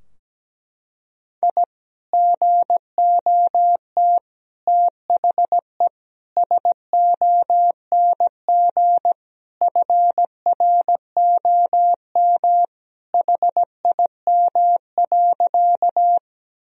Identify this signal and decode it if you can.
Morse code, heard as a keyed tone, about 17 wpm, 700 Hz. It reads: I GOT THE SONG FROM HIM.
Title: Sentences from Top 1000 Words - Review Part 2 - 17wpm